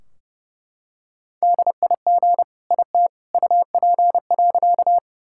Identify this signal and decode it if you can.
Morse code, heard as a keyed tone, about 30 wpm, 700 Hz. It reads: DIG IT UP.
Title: Sentences from Top 1600 Words 30wpm